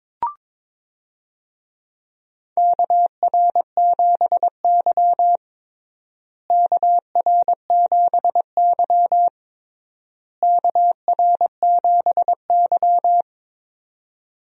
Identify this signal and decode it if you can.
Morse code, heard as a keyed tone, about 22 wpm, 700 Hz. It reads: KR7Y KR7Y KR7Y
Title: USA Call Signs - 4 Characters - Repeated 3x - 22wpm